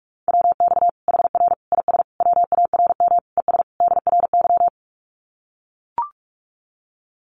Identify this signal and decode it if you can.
Morse code, heard as a keyed tone, about 44 wpm, 700 Hz. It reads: WX HR IS WARM ES DRY
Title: Common QSO Phrases 45wpm